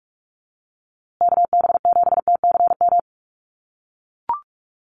Morse code, heard as a keyed tone, about 44 wpm, 700 Hz. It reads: KB7TCM
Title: USA Call Signs Repeated 2x 45wpm